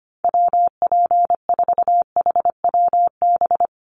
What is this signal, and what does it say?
Morse code, heard as a keyed tone, about 25 wpm, 700 Hz. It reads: WP4HWB
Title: USA Call Signs Repeated 2x 25wpm